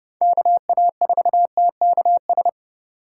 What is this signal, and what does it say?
Morse code, heard as a keyed tone, about 30 wpm, 700 Hz. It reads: KA4TKS
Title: USA Call Signs - 3x Character Spacing - 30wpm